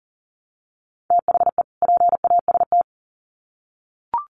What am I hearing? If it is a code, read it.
Morse code, heard as a keyed tone, about 40 wpm, 700 Hz. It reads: THE PAST
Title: Sets of 2 Words 40wpm